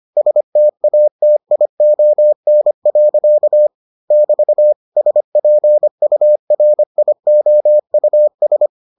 Morse code, heard as a keyed tone, about 25 wpm, 600 Hz. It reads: STATION. <BT> SPURIOUS